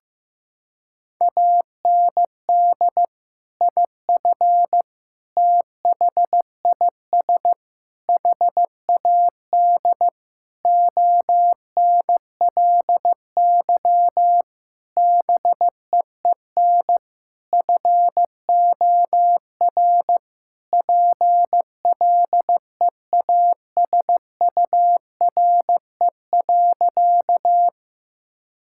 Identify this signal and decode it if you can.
Morse code, heard as a keyed tone, about 15 wpm, 700 Hz. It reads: AND IF THIS HAD ONLY BEEN FOR PLEASURE.
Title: Sentences from Top 1600 Words 15wpm